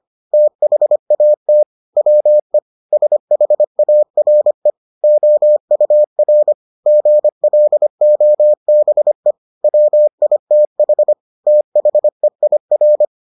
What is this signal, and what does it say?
Morse code, heard as a keyed tone, about 25 wpm, 600 Hz. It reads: THAT WE SHARE OUR GLOBE WITH THEIR